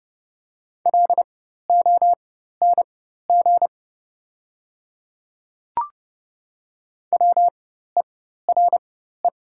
Morse code, heard as a keyed tone, about 30 wpm, spaced as for 14 wpm, 700 Hz. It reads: LONG WERE
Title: 4x Character Spacing - Top 200 Words - 30wpm